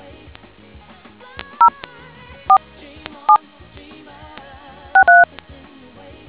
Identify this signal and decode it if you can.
DTMF, heard as telephone touch tones, tones of about 100 ms. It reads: 04763